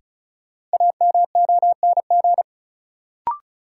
Morse code, heard as a keyed tone, about 36 wpm, 700 Hz. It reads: AMONG
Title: Sets of 2 Words - New Word Drill - 35wpm